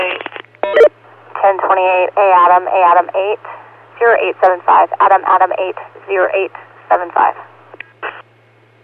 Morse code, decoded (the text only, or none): none